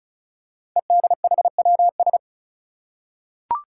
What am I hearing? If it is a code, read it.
Morse code, heard as a keyed tone, about 36 wpm, 700 Hz. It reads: EDHWS
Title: Five Challenging Letter-Number Combinations 35wpm